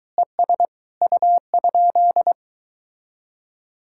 Morse code, heard as a keyed tone, about 22 wpm, 700 Hz. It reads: ES U?